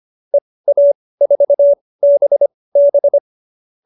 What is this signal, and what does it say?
Morse code, heard as a keyed tone, about 25 wpm, spaced as for 18 wpm, 570 Hz. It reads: EA4BB
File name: qso215b 18